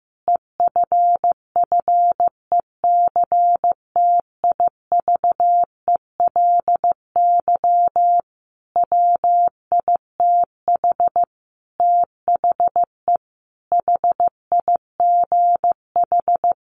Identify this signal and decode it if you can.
Morse code, heard as a keyed tone, about 15 wpm, 700 Hz. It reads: EFFECTIVELY WITH THE HIGH
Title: Sentences from Top 1400 Words 15wpm